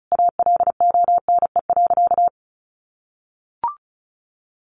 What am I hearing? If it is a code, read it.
Morse code, heard as a keyed tone, about 34 wpm, 700 Hz. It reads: ALONE.
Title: Sentences from Top 1000 Words 35wpm